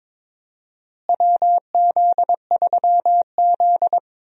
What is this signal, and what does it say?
Morse code, heard as a keyed tone, about 22 wpm, 700 Hz. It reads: WZ3Z
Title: USA Call Signs - 4 Characters - Repeated 2x - 22wpm